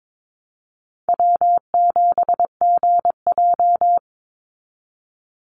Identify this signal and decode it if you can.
Morse code, heard as a keyed tone, about 22 wpm, 700 Hz. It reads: W7GJ
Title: NA SOTA Chaser Callsigns 22wpm